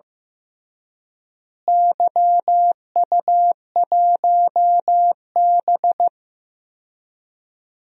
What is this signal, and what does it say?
Morse code, heard as a keyed tone, about 15 wpm, 700 Hz. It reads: YU1B
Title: DXCC-100 Call Signs - 4 Characters - Repeated 3x - 15wpm